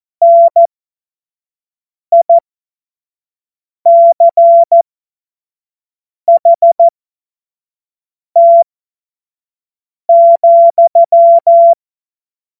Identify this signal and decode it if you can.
Morse code, heard as a keyed tone, about 14 wpm, spaced as for 5 wpm, 680 Hz. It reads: NICHT,
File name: Morsekurs / Witze / Witz 09. November 2021, 25 F70